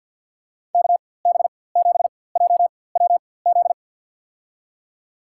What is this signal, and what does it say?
Morse code, heard as a keyed tone, about 53 wpm, spaced as for 24 wpm, 700 Hz. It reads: KB7JWZ
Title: USA Call Signs - 6 Characters - 4x Character Spacing - 50wpm